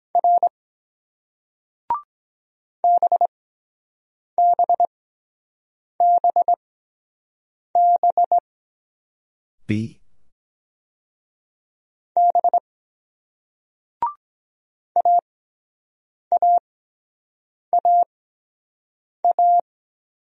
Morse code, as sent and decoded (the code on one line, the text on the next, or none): .-. / -... -... -... -... / -... / .- .- .- .-
R BBBB B AAAA